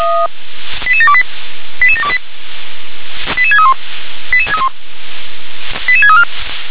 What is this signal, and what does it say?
Five-tone selective call (ZVEI), heard as quick runs of five tones, tones of about 70 ms. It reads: 88728 88728 88631 88634